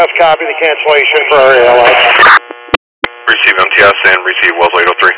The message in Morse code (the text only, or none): none